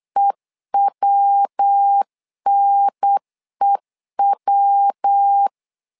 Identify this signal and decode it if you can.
Morse code, heard as a keyed tone, about 8 wpm, 800 Hz. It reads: EWNEW
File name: LEZIONI / 3 MWRO / 3 MW 8wpm 2501